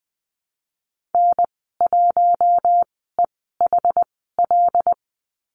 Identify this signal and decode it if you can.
Morse code, heard as a keyed tone, about 20 wpm, spaced as for 15 wpm, 700 Hz. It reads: N1EHL
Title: 082 Lesson 14g - Number 2 - Call Signs 20wpm